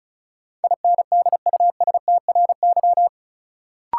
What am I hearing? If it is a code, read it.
Morse code, heard as a keyed tone, about 36 wpm, 700 Hz. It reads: INDUSTRY
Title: Sentences from Top 1000 Words - Review Part 4 - New Word Drill - 35wpm